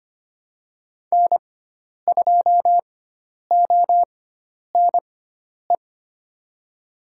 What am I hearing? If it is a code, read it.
Morse code, heard as a keyed tone, about 25 wpm, spaced as for 10 wpm, 700 Hz. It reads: N2ONE
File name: Lesson 16g - Letter c - Call Signs 25wpm